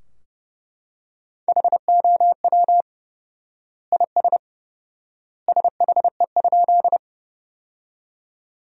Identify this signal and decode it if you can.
Morse code, heard as a keyed tone, about 30 wpm, 700 Hz. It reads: HOW IS SHE?
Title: 4x Word-Spacing - Sentences from Top 100 Words - 30wpm